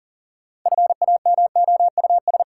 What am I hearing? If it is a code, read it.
Morse code, heard as a keyed tone, about 40 wpm, 700 Hz. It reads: FAMOUS
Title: Sentences from Top 1500 Words - New Word Drill - 40wpm